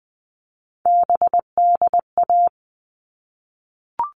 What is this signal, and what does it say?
Morse code, heard as a keyed tone, about 20 wpm, 700 Hz. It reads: BDA
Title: All Cut Numbers - 3-Digit 20wpm